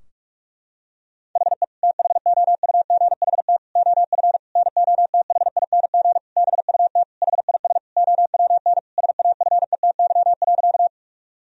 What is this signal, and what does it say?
Morse code, heard as a keyed tone, about 41 wpm, 700 Hz. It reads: HE THOUGHT OF NOTHING BUT HIS OWN SAFETY.
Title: Sentences from Top 1000 Words - Review Part 4 - 45wpm